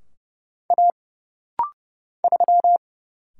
Morse code, heard as a keyed tone, about 30 wpm, 700 Hz. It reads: A 3
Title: Single Letter-Number - Rapid-Fire 30wpm